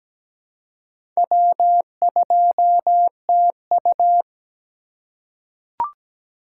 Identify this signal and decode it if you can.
Morse code, heard as a keyed tone, about 17 wpm, 700 Hz. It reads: W2TU
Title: USA Call Signs - 3x Character Spacing - 17wpm